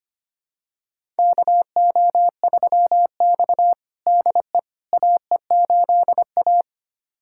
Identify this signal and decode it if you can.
Morse code, heard as a keyed tone, about 25 wpm, 700 Hz. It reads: KO3X DE AE8A
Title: USA Callsign DE Callsign - 4 Characters - 25wpm